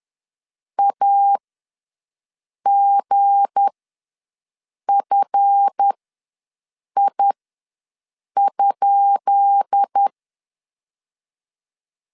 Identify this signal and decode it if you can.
Morse code, heard as a keyed tone, about 10 wpm, spaced as for 6 wpm, 800 Hz. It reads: AGFI?
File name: LEZIONI / 10 SIGNS / 10B INT ws 10wpm 4134